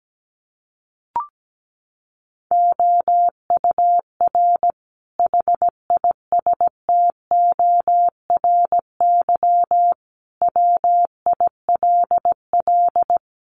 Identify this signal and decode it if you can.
Morse code, heard as a keyed tone, about 17 wpm, 700 Hz. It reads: OUR HISTORY WILL
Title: Sentences from Top 1500 Words 17wpm